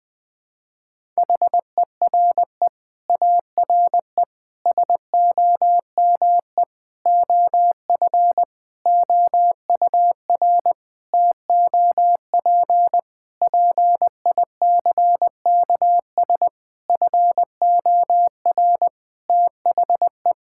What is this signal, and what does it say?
Morse code, heard as a keyed tone, about 20 wpm, 700 Hz. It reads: HERE ARE SOME OF OUR TOP PICKS FOR THE
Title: Sentences with 9-Word Phrases - 20wpm